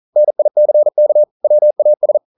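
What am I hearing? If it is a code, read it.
Morse code, heard as a keyed tone, about 42 wpm, 600 Hz. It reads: NICK WAS
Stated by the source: Morse code practice at 41 wpm, from CW med Gnister.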